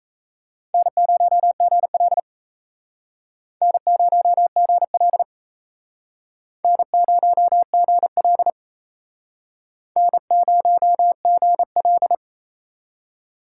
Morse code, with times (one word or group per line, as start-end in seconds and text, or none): 0.7-2.2 N0GL
3.6-5.2 N0GL
6.6-8.5 N0GL
10.0-12.1 N0GL